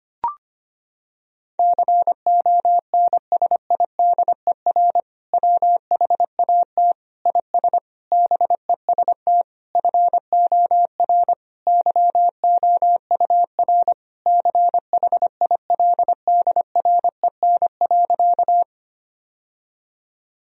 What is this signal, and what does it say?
Morse code, heard as a keyed tone, about 25 wpm, 700 Hz. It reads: CONSIDER WHAT IS BEST FOR YOUR CHILDREN.
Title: Sentences from Top 1000 Words - Review Part 4 - 25wpm